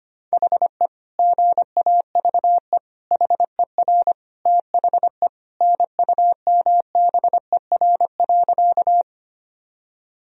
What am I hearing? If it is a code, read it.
Morse code, heard as a keyed tone, about 25 wpm, 700 Hz. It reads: HE GAVE HER THE NUMBER.
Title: Sentences from Top 1000 Words - Review Part 1 - 25wpm